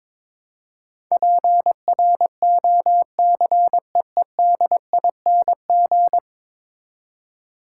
Morse code, heard as a keyed tone, about 21 wpm, 700 Hz. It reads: PROCEEDING
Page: morse-code-ninja-practice-22wpm